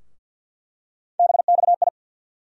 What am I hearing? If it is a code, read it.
Morse code, heard as a keyed tone, about 49 wpm, 700 Hz. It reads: BXI
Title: USA Call Sign Postfixes - 3 Letters 50wpm